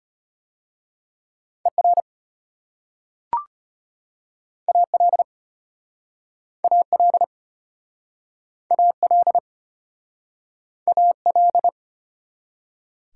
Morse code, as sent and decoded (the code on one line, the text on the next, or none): . .-. / .- .-.. / .- .-.. / .- .-.. / .- .-..
ER AL AL AL AL